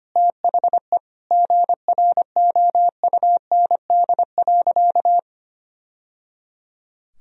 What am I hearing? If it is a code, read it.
Morse code, heard as a keyed tone, about 25 wpm, 700 Hz. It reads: THE GROUND.
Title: Sentences from Top 1000 Words - Review Part 2 - 25wpm